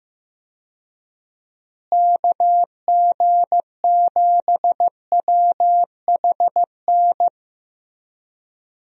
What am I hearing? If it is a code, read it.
Morse code, heard as a keyed tone, about 15 wpm, 700 Hz. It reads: KG7WHN
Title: USA Call Signs - 6 Characters - Repeated 2x - 15wpm